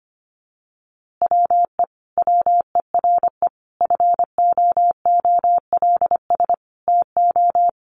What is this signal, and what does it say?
Morse code, heard as a keyed tone, about 25 wpm, 700 Hz. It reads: WE WERE FOOLS TO
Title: A Sound Of Thunder 25wpm